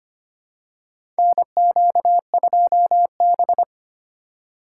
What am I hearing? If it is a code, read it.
Morse code, heard as a keyed tone, about 25 wpm, 700 Hz. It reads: NQ2B
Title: USA Call Signs - 4 Characters - 6x Character Spacing - 25wpm